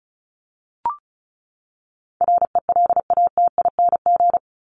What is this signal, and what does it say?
Morse code, heard as a keyed tone, about 34 wpm, 700 Hz. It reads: RELATING
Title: ING Suffix - Top 100 Words with Base Words 35wpm